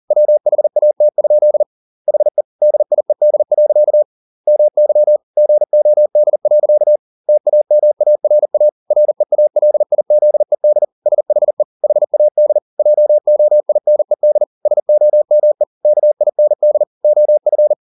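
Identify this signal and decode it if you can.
Morse code, heard as a keyed tone, about 40 wpm, 600 Hz. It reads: WHAT? HE DIED. MY GOD. TAMARA REALIZED SHE HAD JOINED SOME KIND OF
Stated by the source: Morse code at 40 wpm, practice text, CW med Gnister.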